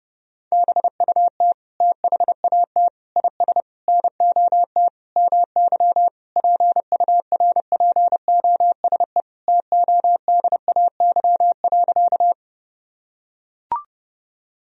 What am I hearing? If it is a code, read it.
Morse code, heard as a keyed tone, about 30 wpm, 700 Hz. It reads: BUT THAT IS NOT MY PURPOSE TODAY.
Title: Sentences from Top 1000 Words - Review Part 4 - 30wpm